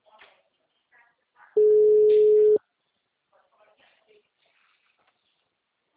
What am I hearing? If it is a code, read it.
Ringback tone (Europe, 425 Hz), with speech.
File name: out- -2002- - - .13939